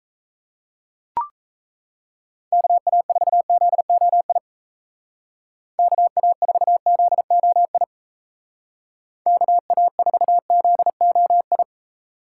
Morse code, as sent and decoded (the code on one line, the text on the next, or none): -.- .- ....- --.. --- .. / -.- .- ....- --.. --- .. / -.- .- ....- --.. --- ..
KA4ZOI KA4ZOI KA4ZOI